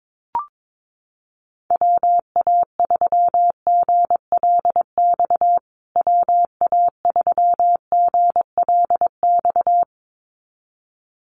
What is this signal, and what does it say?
Morse code, heard as a keyed tone, about 22 wpm, 700 Hz. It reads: WA3GLX WA3GLX
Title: USA Call Signs Repeated 2x 22wpm